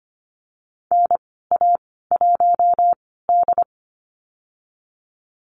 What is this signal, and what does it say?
Morse code, heard as a keyed tone, about 25 wpm, spaced as for 16 wpm, 700 Hz. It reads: NA1D
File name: Lesson 15g - Number 5 - Call Signs 25wpm